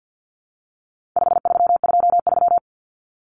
Morse code, heard as a keyed tone, about 50 wpm, 700 Hz. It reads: 5323
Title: Numbers - 4-Digit 50wpm